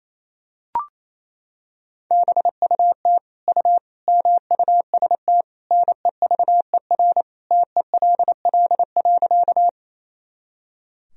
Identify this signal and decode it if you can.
Morse code, heard as a keyed tone, about 28 wpm, 700 Hz. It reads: BUT U MUST NEVER TELL.
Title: Sentences from Top 300 Words and Abbreviations 28wpm